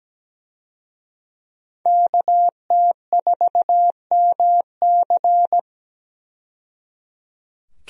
Morse code, heard as a keyed tone, about 17 wpm, 700 Hz. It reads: KT4MC
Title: USA Call Signs - 5 Characters - 17wpm